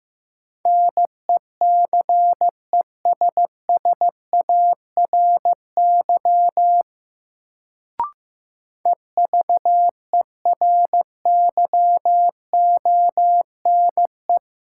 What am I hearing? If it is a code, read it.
Morse code, heard as a keyed tone, about 15 wpm, 700 Hz. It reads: NECESSARY EVERYONE
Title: Sentences from Top 800 Words - New Word Drill - 15wpm